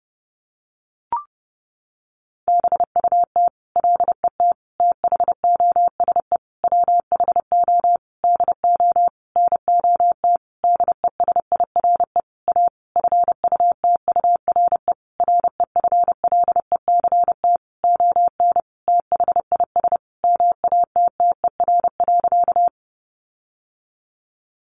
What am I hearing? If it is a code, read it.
Morse code, heard as a keyed tone, about 30 wpm, 700 Hz. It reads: BUT LET THOSE WHO DO NOT DESIRE A FUTURE REFLECT ON THIS MATTER.
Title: Sentences from Top 1500 Words 30wpm